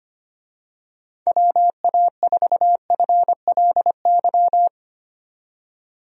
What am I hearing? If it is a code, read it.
Morse code, heard as a keyed tone, about 25 wpm, 700 Hz. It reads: WA4FLY
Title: USA Call Signs - 6 Characters - 25wpm